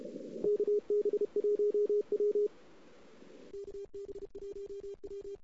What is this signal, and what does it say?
Morse code, heard as a keyed tone, about 32 wpm, 410 Hz. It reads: KB1W KB1W
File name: CW / 2008-ss-cw-k5tr-op-n5rz -03:00